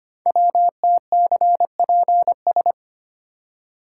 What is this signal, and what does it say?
Morse code, heard as a keyed tone, about 25 wpm, 700 Hz. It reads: WTCPS
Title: Code Groups - 5 Letters - 25wpm